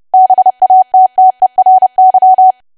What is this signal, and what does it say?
Morse code, heard as a keyed tone, about 30 wpm, 750 Hz. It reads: BATTERY
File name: cwa / W1AW-Level3 / 30WPM